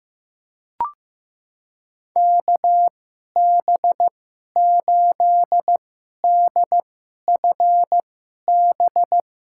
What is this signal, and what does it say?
Morse code, heard as a keyed tone, about 15 wpm, spaced as for 11 wpm, 700 Hz. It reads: KB8DFB
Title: USA Call Signs - 6 Characters - 2x Character Spacing - 15wpm